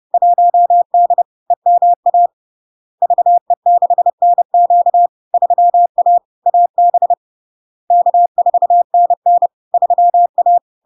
Morse code, heard as a keyed tone, about 30 wpm, 700 Hz. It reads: 1D EMA VE6NQ 3A AB K4NN 3A
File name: LARK-FD-2022-Log-in-CW-30wpm-700-Hz